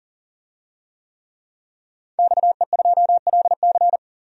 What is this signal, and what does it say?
Morse code, heard as a keyed tone, about 40 wpm, 700 Hz. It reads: XE2LC
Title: DXCC-100 Call Signs - 5 Characters - 40wpm